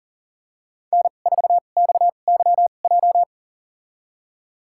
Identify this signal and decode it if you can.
Morse code, heard as a keyed tone, about 40 wpm, spaced as for 29 wpm, 700 Hz. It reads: N4XYJ
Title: USA Call Signs - 5 Characters - 2x Character Spacing - 40wpm